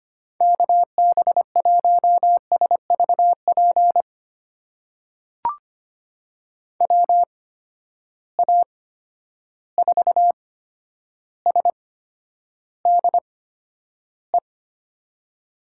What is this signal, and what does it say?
Morse code, heard as a keyed tone, about 25 wpm, spaced as for 11 wpm, 700 Hz. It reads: KB1SVP WA4SDE
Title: USA Call Signs - 6 Characters - 8x Character Spacing - 25wpm